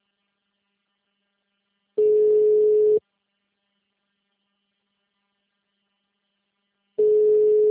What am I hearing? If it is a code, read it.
Ringback tone (Europe, 425 Hz).